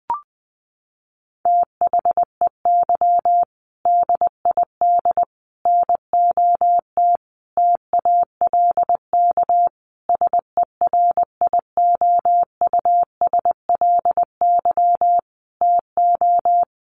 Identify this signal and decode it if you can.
Morse code, heard as a keyed tone, about 20 wpm, 700 Hz. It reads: THEY DID NOT TALK SERIOUSLY TO
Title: Sentences from Top 1200 Words 20wpm